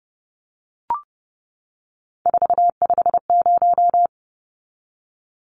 Morse code, heard as a keyed tone, about 30 wpm, 700 Hz. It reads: 450
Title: Numbers - 3-Digit 30wpm